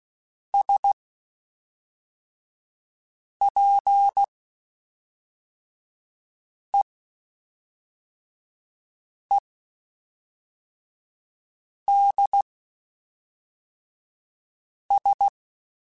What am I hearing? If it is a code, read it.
Morse code, heard as a keyed tone, about 16 wpm, spaced as for 3 wpm, 800 Hz. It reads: SPEEDS
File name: Spaced Characters / Spaced Characters-mp3s / Weather 4 10 sp @5